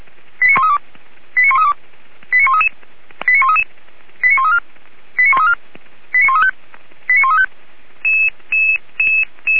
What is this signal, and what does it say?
Five-tone selective call (ZVEI), heard as quick runs of five tones, tones of about 70 ms. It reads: 89132 89132 89133 89133 89134 89134 89135 89135